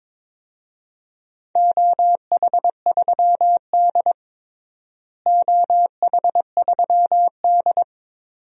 Morse code, heard as a keyed tone, about 22 wpm, 700 Hz. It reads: OH3D OH3D
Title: DXCC-100 Call Signs - 4 Characters - Repeated 2x - 22wpm